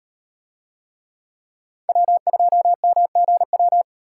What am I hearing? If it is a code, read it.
Morse code, heard as a keyed tone, about 38 wpm, 700 Hz. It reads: W2MGW